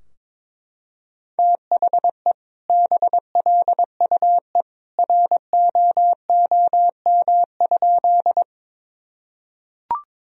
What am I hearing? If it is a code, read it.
Morse code, heard as a keyed tone, about 23 wpm, 700 Hz. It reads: THE BLUE ROOM?
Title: Sentences from Top 1000 Words - Review Part 3 - 22wpm